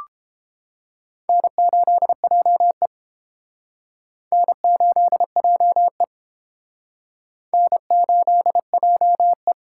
Morse code, heard as a keyed tone, about 30 wpm, 700 Hz. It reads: N8JE N8JE N8JE